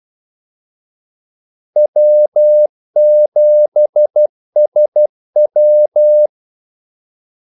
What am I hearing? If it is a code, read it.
Morse code, heard as a keyed tone, about 12 wpm, 600 Hz. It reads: W7SW
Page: 2 ekstra ordmellemrum mellem kaldesignalerne